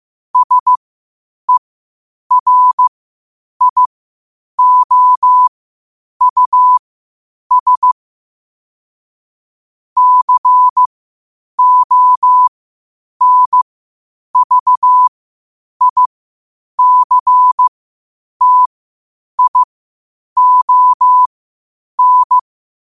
Morse code, heard as a keyed tone, about 15 wpm, spaced as for 8 wpm, 1000 Hz. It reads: SERIOUS CONVICTION
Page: qotd-at-8-wpm